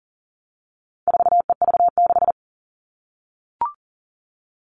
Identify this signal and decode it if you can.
Morse code, heard as a keyed tone, about 40 wpm, 700 Hz. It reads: 4EV6
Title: All Cut Numbers - 4-Digit 40wpm